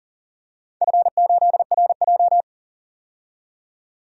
Morse code, heard as a keyed tone, about 42 wpm, 700 Hz. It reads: F8RJ
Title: DXCC-100 Call Signs - 4 Characters - 40wpm